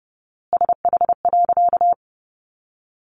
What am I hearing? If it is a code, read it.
Morse code, heard as a keyed tone, about 30 wpm, 700 Hz. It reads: SH.
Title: A Sound Of Thunder 30wpm